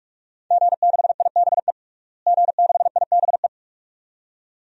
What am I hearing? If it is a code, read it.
Morse code, heard as a keyed tone, about 43 wpm, 700 Hz. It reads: G6IBE G6IBE
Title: DXCC-100 Call Signs - 5 Characters - Repeated 2x - 45wpm